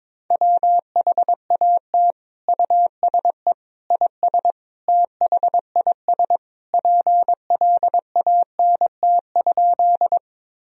Morse code, heard as a keyed tone, about 21 wpm, 700 Hz. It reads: WHAT USE IS THIS PLANT?
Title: Sentences from Top 1000 Words - Review Part 1 - 22wpm